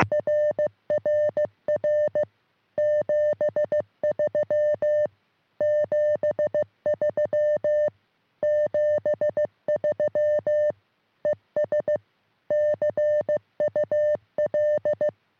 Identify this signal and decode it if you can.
Morse code, heard as a keyed tone, about 15 wpm, 600 Hz. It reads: RRR 73 73 73 ES CUL
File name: sm7lcb / g4bao all 23cm